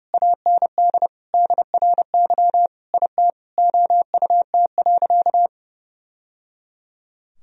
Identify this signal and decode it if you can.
Morse code, heard as a keyed tone, about 30 wpm, 700 Hz. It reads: AND DRY IT OUT.
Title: Sentences from Top 1000 Words - Review Part 3 - 30wpm